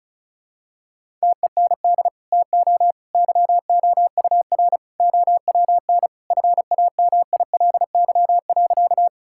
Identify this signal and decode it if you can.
Morse code, heard as a keyed tone, about 36 wpm, 700 Hz. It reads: TEND TO YOUR OWN FAMILY.
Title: Sentences from Top 1000 Words - Review Part 5 - 35wpm